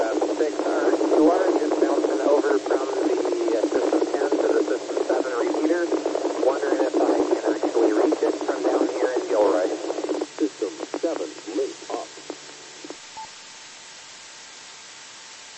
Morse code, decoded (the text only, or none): none